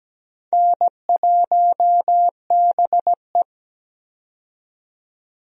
Morse code, heard as a keyed tone, about 17 wpm, 700 Hz. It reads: N1BE
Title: USA Call Signs - 4 Characters - 17wpm